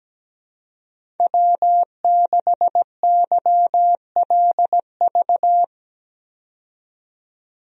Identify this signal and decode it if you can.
Morse code, heard as a keyed tone, about 17 wpm, 700 Hz. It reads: W6YLV
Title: USA Call Signs - 5 Characters - 17wpm